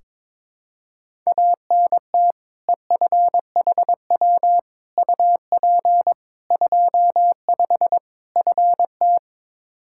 Morse code, heard as a keyed tone, about 21 wpm, 700 Hz. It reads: ANT EFHW UP 25 FT
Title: K1USN SST QSO Practice 22wpm